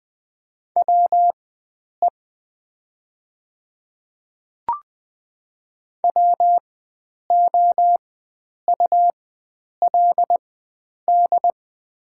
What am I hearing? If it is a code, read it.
Morse code, heard as a keyed tone, about 20 wpm, spaced as for 9 wpm, 700 Hz. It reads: WE WOULD
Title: 4x Character Spacing - Top 100 Words - 20wpm